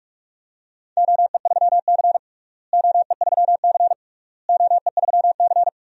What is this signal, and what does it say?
Morse code, heard as a keyed tone, about 46 wpm, 700 Hz. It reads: OE3C OE3C OE3C
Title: DXCC-100 Call Signs - 4 Characters - Repeated 3x - 45wpm